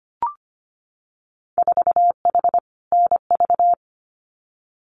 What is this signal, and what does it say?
Morse code, heard as a keyed tone, about 25 wpm, 690 Hz. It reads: 4H NV
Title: Winter Field Day Exchanges Encore 25wpm